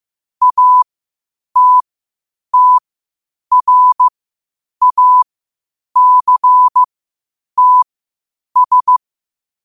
Morse code, heard as a keyed tone, about 15 wpm, spaced as for 8 wpm, 1000 Hz. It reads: ATTRACTS
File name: qotdfiles / 8wpm